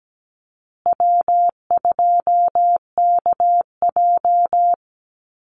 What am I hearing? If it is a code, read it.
Morse code, heard as a keyed tone, about 17 wpm, 700 Hz. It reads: W2KJ
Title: Easy Call Signs Repeated 3x - 17wpm